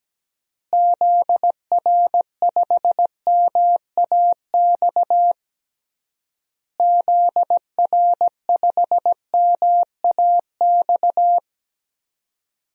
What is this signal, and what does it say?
Morse code, heard as a keyed tone, about 17 wpm, 700 Hz. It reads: ZR5MAX ZR5MAX
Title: DXCC-100 Call Signs - 6 Characters - Repeated 3x - 17wpm